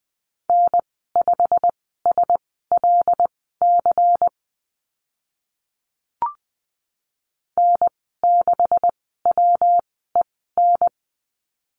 Morse code, heard as a keyed tone, about 20 wpm, spaced as for 15 wpm, 700 Hz. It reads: N5SLC N6WEN
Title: 167 Lesson 27g - Number 7 - Call Signs 20wpm